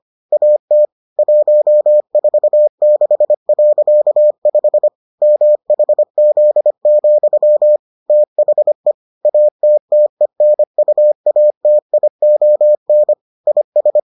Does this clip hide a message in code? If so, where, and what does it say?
Yes: Morse code from 0.3 to 14.0 s, heard as a keyed tone, about 25 wpm, 600 Hz. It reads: AT 146.5 MHZ, THE ATTENUATION IS